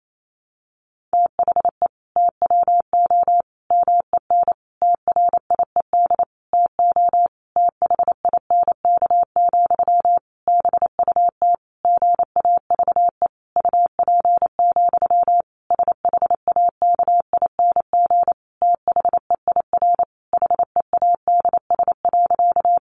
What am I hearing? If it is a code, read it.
Morse code, heard as a keyed tone, about 28 wpm, 700 Hz. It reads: THE TWO MEN TRIED TO THINK, BUT GAVE UP, SHAKING THEIR HEADS.
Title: A Sound Of Thunder 28wpm